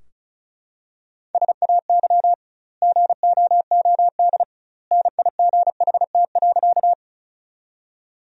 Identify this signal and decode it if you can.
Morse code, heard as a keyed tone, about 36 wpm, 700 Hz. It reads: SAY GOOD NIGHT.
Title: 2x Word-Spacing - Sentences from Top 300 Words - 35wpm